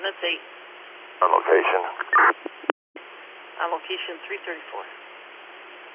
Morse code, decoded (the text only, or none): none